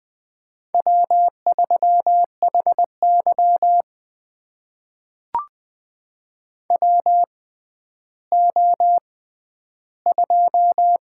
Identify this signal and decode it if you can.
Morse code, heard as a keyed tone, about 20 wpm, spaced as for 11 wpm, 700 Hz. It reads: W3HY WO2
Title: USA Call Signs - 4 Characters - 6x Character Spacing - 20wpm